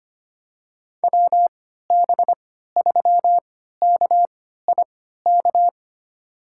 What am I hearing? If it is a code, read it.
Morse code, heard as a keyed tone, about 25 wpm, spaced as for 14 wpm, 700 Hz. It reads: WB3KIK
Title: USA Call Signs - 6 Characters - 3x Character Spacing - 25wpm